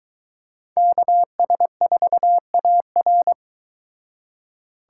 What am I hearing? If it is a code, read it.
Morse code, heard as a keyed tone, about 22 wpm, 700 Hz. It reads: KS4AR